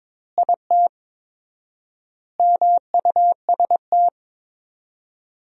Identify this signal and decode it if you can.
Morse code, heard as a keyed tone, about 22 wpm, 700 Hz. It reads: IT MUST
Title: 4x Word-Spacing - Sentences from Top 500 Words - 22wpm